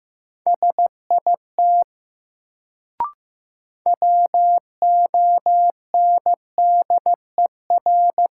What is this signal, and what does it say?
Morse code, heard as a keyed tone, about 15 wpm, 700 Hz. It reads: SIT WONDER
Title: Sentences from Top 500 Words - New Word Drill - 15wpm